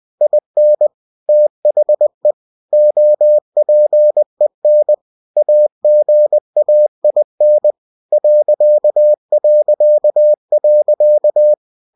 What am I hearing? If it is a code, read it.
Morse code, heard as a keyed tone, about 20 wpm, 600 Hz. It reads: IN THE OPEN AGAIN ...
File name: cw-books / winnie-the-pooh / winnie-20wpm-page-0016